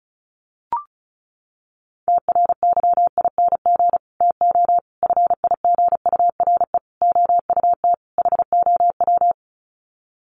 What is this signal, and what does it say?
Morse code, heard as a keyed tone, about 34 wpm, 700 Hz. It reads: TRYING TO FIGURE OUT HOW
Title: Sets of 5 Words Encore 35wpm